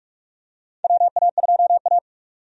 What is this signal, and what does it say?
Morse code, heard as a keyed tone, about 46 wpm, 700 Hz. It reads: WA2A
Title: USA Call Signs - 4 Characters - 45wpm